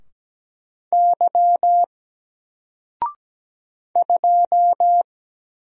Morse code, heard as a keyed tone, about 17 wpm, 700 Hz. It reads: Y 2
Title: Single Letter-Number 17wpm